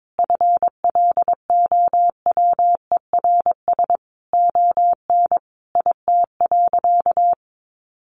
Morse code, heard as a keyed tone, about 22 wpm, 700 Hz. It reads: FLOWERS ON IT.
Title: Additional Easy Phrases 22wpm